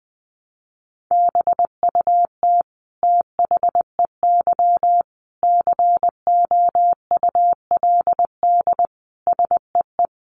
Morse code, heard as a keyed tone, about 20 wpm, 700 Hz. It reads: BUT THEY COULD SEE
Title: Sentences from Top 500 Words 20wpm